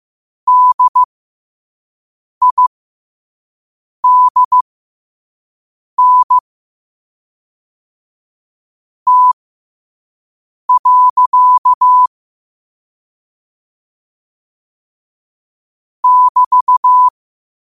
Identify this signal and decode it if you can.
Morse code, heard as a keyed tone, about 15 wpm, spaced as for 5 wpm, 1000 Hz. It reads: DIDN T. <BT>
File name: qotdfiles / 5wpm